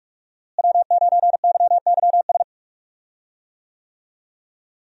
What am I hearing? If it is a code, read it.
Morse code, heard as a keyed tone, about 46 wpm, 700 Hz. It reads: W9YYS
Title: USA Call Signs - 5 Characters - Repeated 2x - 45wpm